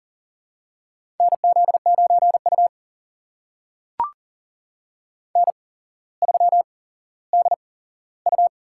Morse code, heard as a keyed tone, about 40 wpm, spaced as for 17 wpm, 700 Hz. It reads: NZ9U N3DU
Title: USA Call Signs - 4 Characters - 8x Character Spacing - 40wpm